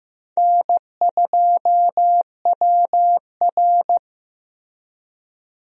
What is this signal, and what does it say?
Morse code, heard as a keyed tone, about 15 wpm, 700 Hz. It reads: N2WR
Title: USA Call Signs - 4 Characters - Repeated 3x - 15wpm